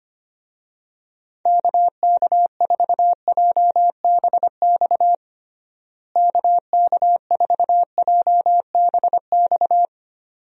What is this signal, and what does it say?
Morse code, heard as a keyed tone, about 25 wpm, 700 Hz. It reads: KK4JBX KK4JBX
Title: USA Call Signs - 6 Characters - Repeated 2x - 25wpm